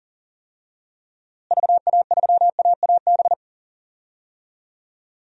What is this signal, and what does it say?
Morse code, heard as a keyed tone, about 40 wpm, 700 Hz. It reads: VA3AAB
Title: DXCC-100 Call Signs - 6 Characters - Repeated 3x - 40wpm